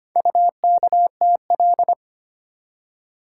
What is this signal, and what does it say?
Morse code, heard as a keyed tone, about 25 wpm, 700 Hz. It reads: UKTL
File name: practice / 25wpm / Four Letter-Number Combinations - 25wpm